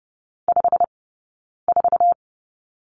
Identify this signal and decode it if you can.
Morse code, heard as a keyed tone, about 30 wpm, 700 Hz. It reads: 5 4
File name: Lesson 19d - Letter w - Two Characters 30wpm